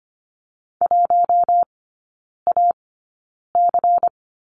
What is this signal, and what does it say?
Morse code, heard as a keyed tone, about 25 wpm, 700 Hz. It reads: 1 A C
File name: Lesson 17e - Letter u - Three Characters 25wpm